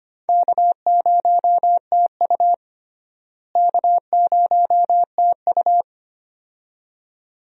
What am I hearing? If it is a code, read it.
Morse code, heard as a keyed tone, about 25 wpm, 700 Hz. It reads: K0TU K0TU
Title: USA Call Signs - 4 Characters - Repeated 2x - 25wpm